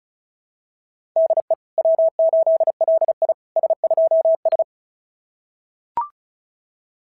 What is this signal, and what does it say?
Morse code, heard as a keyed tone, about 36 wpm, 690 Hz. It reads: DE W8LI S2S
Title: SOTA QSO Practice 35wpm